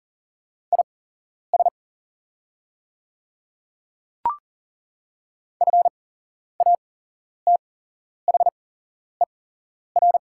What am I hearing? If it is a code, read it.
Morse code, heard as a keyed tone, about 38 wpm, spaced as for 11 wpm, 700 Hz. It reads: IS FATHER
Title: 8x Character Spacing - Top 200 Words - 40wpm